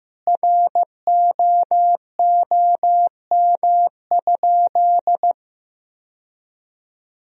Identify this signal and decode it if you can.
Morse code, heard as a keyed tone, about 15 wpm, 700 Hz. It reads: ROOM?
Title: Sentences from Top 1000 Words - Review Part 4 - 15wpm